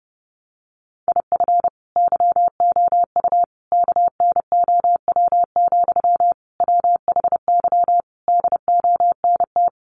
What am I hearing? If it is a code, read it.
Morse code, heard as a keyed tone, about 30 wpm, 700 Hz. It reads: IF YOU KNOW, WHY DONT
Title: Sentences from Top 300 Words 30wpm